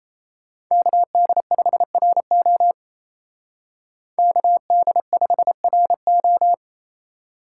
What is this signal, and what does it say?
Morse code, heard as a keyed tone, about 31 wpm, 700 Hz. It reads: KD5RO KD5RO
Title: Speed Racing - USA 2x2 Call Signs 28wpm